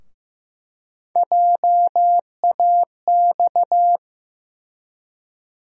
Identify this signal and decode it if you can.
Morse code, heard as a keyed tone, about 15 wpm, 700 Hz. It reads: JAX
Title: USA Call Sign Postfixes - 3 Letters 15wpm